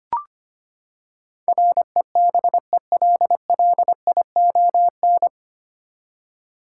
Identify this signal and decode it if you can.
Morse code, heard as a keyed tone, about 25 wpm, 700 Hz. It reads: REBELLION
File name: Nine Letter Words - 25wpm